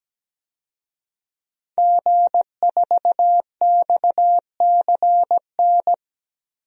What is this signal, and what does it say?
Morse code, heard as a keyed tone, about 17 wpm, 700 Hz. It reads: G4XCN
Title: DXCC-100 Call Signs - 5 Characters - Repeated 2x - 17wpm